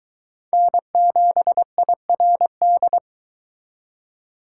Morse code, heard as a keyed tone, about 23 wpm, 700 Hz. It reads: N7IRD